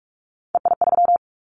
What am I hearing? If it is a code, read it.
Morse code, heard as a keyed tone, about 46 wpm, 700 Hz. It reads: EI3
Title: Callsign Prefixes for Top 50 DXCC - 45wpm